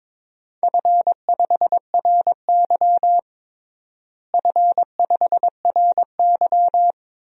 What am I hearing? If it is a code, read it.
Morse code, heard as a keyed tone, about 22 wpm, 700 Hz. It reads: F5RY F5RY
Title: DXCC-100 Call Signs - 4 Characters - Repeated 2x - 22wpm